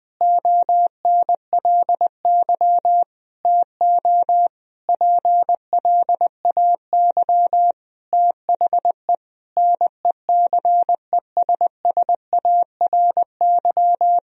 Morse code, heard as a keyed tone, about 21 wpm, 700 Hz. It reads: ONLY TO PLAY THE NECESSARY
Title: Sentences from Top 1000 Words - Review Part 1 - 20wpm